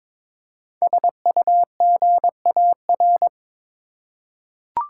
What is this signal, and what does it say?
Morse code, heard as a keyed tone, about 22 wpm, 700 Hz. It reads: SUGAR
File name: 5 Letter Words - First 3 Letters Given - 22wpm